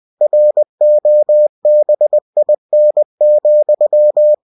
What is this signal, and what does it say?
Morse code, heard as a keyed tone, about 20 wpm, 600 Hz. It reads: ROBIN,
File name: cw-books / winnie-the-pooh / winnie-20wpm-page-0043